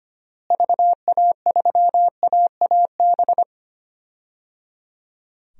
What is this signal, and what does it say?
Morse code, heard as a keyed tone, about 25 wpm, 700 Hz. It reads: VA3AAB
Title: DXCC-100 Call Signs - 6 Characters - Repeated 3x - 25wpm